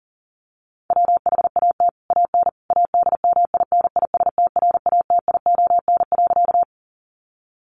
Morse code, heard as a keyed tone, about 37 wpm, 700 Hz. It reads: WHAT AN ADMINISTRATION.
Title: Sentences from Top 1300 Words 40wpm